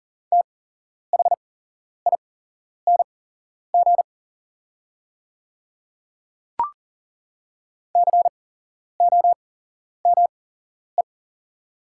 Morse code, heard as a keyed tone, about 40 wpm, spaced as for 11 wpm, 690 Hz. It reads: THING COME
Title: 8x Character Spacing - Top 100 Words - 40wpm